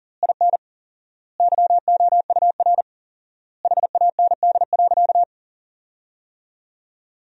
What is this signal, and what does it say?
Morse code, heard as a keyed tone, about 37 wpm, 700 Hz. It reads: IN YOUR HAND.
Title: 4x Word-Spacing - Sentences from Top 400 Words - 40wpm